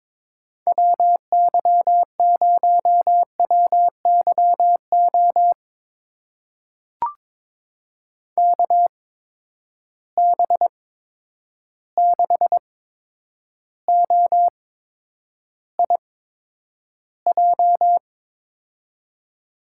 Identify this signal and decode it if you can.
Morse code, heard as a keyed tone, about 23 wpm, spaced as for 10 wpm, 700 Hz. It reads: WY0WYO KB6OIJ
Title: USA Call Signs - 6 Characters - 8x Character Spacing - 22wpm